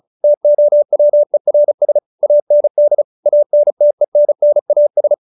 Morse code, heard as a keyed tone, about 34 wpm, 590 Hz. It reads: TOWERS AND ANTENNAS